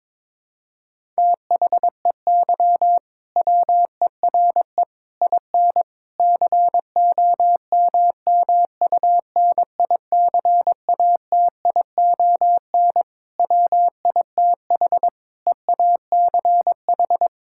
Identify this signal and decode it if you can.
Morse code, heard as a keyed tone, about 22 wpm, 700 Hz. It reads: THEY WERE IN COMMUNICATION WITH EACH
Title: Sentences from Top 1000 Words - Review Part 6 - 22wpm